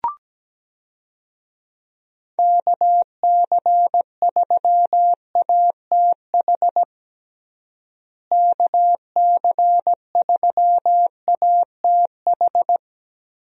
Morse code, heard as a keyed tone, about 17 wpm, 700 Hz. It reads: KC3ATH KC3ATH
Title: USA Call Signs - 6 Characters - Repeated 2x - 17wpm